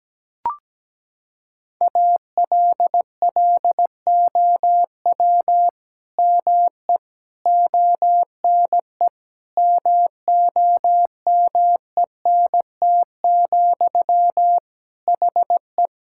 Morse code, heard as a keyed tone, about 17 wpm, 700 Hz. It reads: ALLOW ME ONE MOMENT, HE
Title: Sentences from Top 1000 Words - Review Part 5 - 17wpm